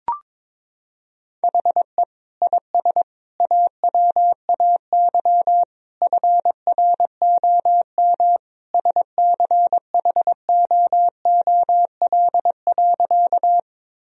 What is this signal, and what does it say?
Morse code, heard as a keyed tone, about 22 wpm, 700 Hz. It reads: HE IS AWAY FROM SCHOOL.
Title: Sentences from Top 1000 Words - Review Part 1 - 22wpm